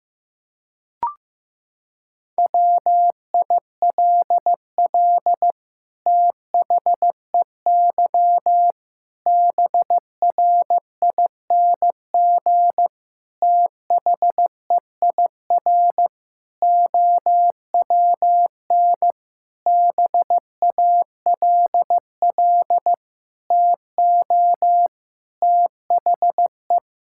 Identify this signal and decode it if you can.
Morse code, heard as a keyed tone, about 15 wpm, 700 Hz. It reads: WILL THEY BRING THEIR OWN BALL TO THE
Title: Sentences from Top 1000 Words - Review Part 6 - 15wpm